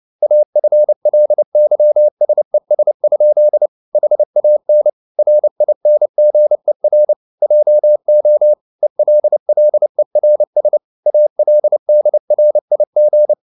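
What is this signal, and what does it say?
Morse code, heard as a keyed tone, about 29 wpm, 600 Hz. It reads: AFLYSES? HAN RINGER JO ELLERS ALDRIG
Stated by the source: Fast (QRQ) Morse at 29 wpm, from CW med Gnister.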